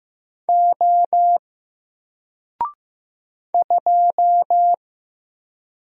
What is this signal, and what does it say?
Morse code, heard as a keyed tone, about 15 wpm, 700 Hz. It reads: O 2
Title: Single Character (Letter:Number:Punctuation:Prosign) 15wpm